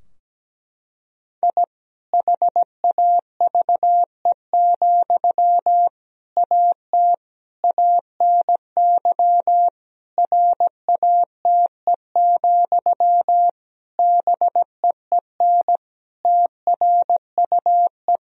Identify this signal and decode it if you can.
Morse code, heard as a keyed tone, about 17 wpm, 700 Hz. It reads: I HAVE, AT ANY RATE, BEEN TRUE
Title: Sentences from Top 1000 Words - Review Part 4 - 17wpm